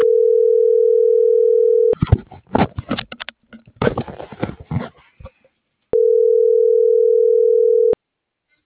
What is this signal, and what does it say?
Ringback tone (North America), 440 + 480 Hz.